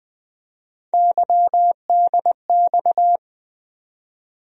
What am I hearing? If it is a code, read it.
Morse code, heard as a keyed tone, about 21 wpm, 700 Hz. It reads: YDX
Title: USA Call Sign Postfixes - 3 Letters 20wpm